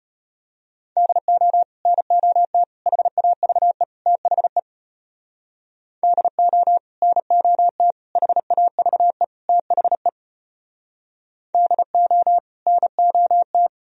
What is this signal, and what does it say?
Morse code, heard as a keyed tone, about 33 wpm, 700 Hz. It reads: DO NOT HAVE THE DO NOT HAVE THE DO NOT